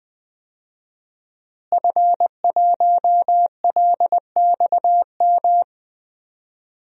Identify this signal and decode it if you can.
Morse code, heard as a keyed tone, about 21 wpm, 700 Hz. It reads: F1LXM
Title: DXCC-100 Call Signs - 5 Characters - Repeated 2x - 20wpm